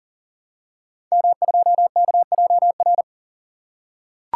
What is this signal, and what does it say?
Morse code, heard as a keyed tone, about 40 wpm, 690 Hz. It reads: M2KJR
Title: Code Groups - 5 Letter:Numbers - 40wpm